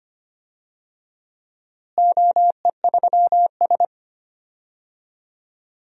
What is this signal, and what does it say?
Morse code, heard as a keyed tone, about 25 wpm, 700 Hz. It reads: OE3S
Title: DXCC-100 Call Signs - 4 Characters - Repeated 3x - 25wpm